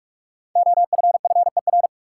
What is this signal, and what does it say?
Morse code, heard as a keyed tone, about 41 wpm, 700 Hz. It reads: OFFER
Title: Sentences from Top 900 Words - New Word Drill - 45wpm